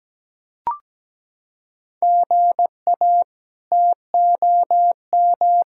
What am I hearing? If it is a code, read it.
Morse code, heard as a keyed tone, about 17 wpm, 700 Hz. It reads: GA TOM
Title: K1USN SST QSO Practice - 17wpm